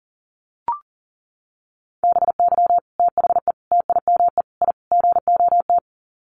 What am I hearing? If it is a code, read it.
Morse code, heard as a keyed tone, about 40 wpm, 700 Hz. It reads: BY THE TIME I GOT
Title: Sets of 5 Words Encore2 40wpm